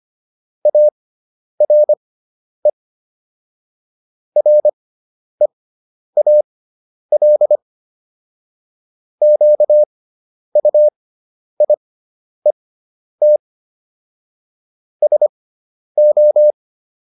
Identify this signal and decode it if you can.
Morse code, heard as a keyed tone, about 25 wpm, spaced as for 10 wpm, 600 Hz. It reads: ARE REAL QUIET SO